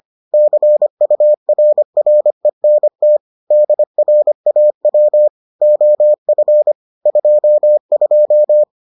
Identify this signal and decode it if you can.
Morse code, heard as a keyed tone, about 25 wpm, 600 Hz. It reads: CURRENT DRAW OF 22